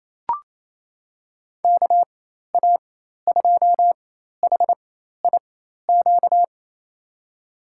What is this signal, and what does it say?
Morse code, heard as a keyed tone, about 28 wpm, spaced as for 13 wpm, 700 Hz. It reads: KA2HIQ
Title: USA Call Signs - 6 Characters - 4x Character Spacing - 28wpm